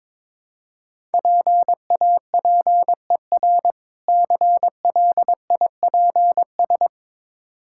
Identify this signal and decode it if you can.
Morse code, heard as a keyed tone, about 22 wpm, 690 Hz. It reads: PAPER CLIPS
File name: Difficult Words with Hints - 22wpm